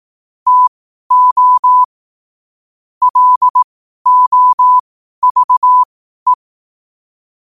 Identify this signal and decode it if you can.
Morse code, heard as a keyed tone, about 18 wpm, spaced as for 12 wpm, 1000 Hz. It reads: TO LOVE
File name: qotdfiles / 12wpm